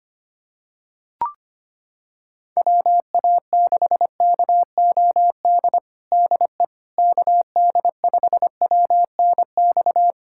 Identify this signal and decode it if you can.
Morse code, heard as a keyed tone, about 25 wpm, 700 Hz. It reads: WA6KOD DE KD5WNX
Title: USA Callsign DE Callsign - 6 Characters - 25wpm